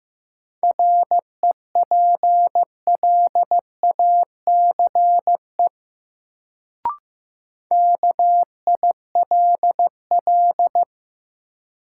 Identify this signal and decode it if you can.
Morse code, heard as a keyed tone, about 15 wpm, 700 Hz. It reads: REPLACE KILL
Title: Sentences from Top 1400 Words - New Word Drill - 15wpm